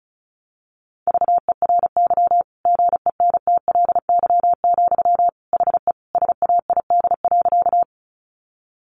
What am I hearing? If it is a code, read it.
Morse code, heard as a keyed tone, about 34 wpm, 700 Hz. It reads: VERY GENTLY, HE SAID.
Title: Sentences from Top 1500 Words 35wpm